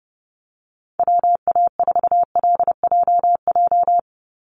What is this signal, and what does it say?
Morse code, heard as a keyed tone, about 30 wpm, 700 Hz. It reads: WA4LJJ
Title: Call Signs Repeated 3x - 30wpm